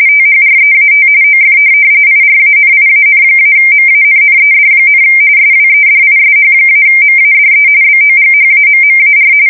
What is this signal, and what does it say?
RTTY, about 45.45 baud. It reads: THREAT OF IRANIAN-MANNED, SWEDISH-MADE BOGHAMMAR
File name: fsk170 / USS-Missouri